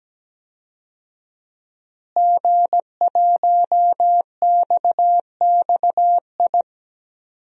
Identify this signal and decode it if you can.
Morse code, heard as a keyed tone, about 17 wpm, 700 Hz. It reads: G1XXI
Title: DXCC-100 Call Signs - 5 Characters - Repeated 2x - 17wpm